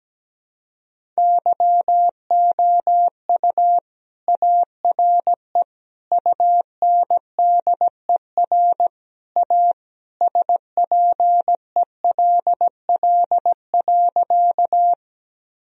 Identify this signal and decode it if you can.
Morse code, heard as a keyed tone, about 17 wpm, 700 Hz. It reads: YOU ARE UNDER A SPELL.
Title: Sentences from Top 1000 Words - Review Part 1 - 17wpm